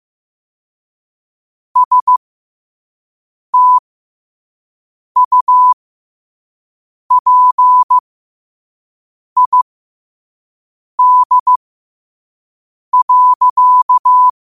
Morse code, heard as a keyed tone, about 15 wpm, spaced as for 5 wpm, 1000 Hz. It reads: STUPID.
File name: qotdfiles / 5wpm